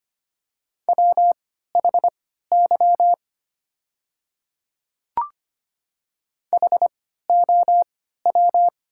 Morse code, heard as a keyed tone, about 25 wpm, spaced as for 14 wpm, 690 Hz. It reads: WHY HOW
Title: 3x Character Spacing - Top 300 Words - 25wpm